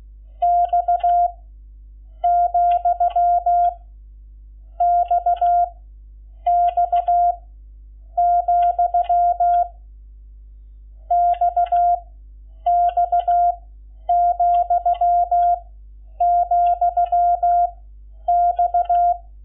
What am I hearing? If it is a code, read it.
Morse code, heard as a keyed tone, about 16 wpm, 700 Hz. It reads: X , X X , X X , , X
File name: Lecciones / CLASE Nº11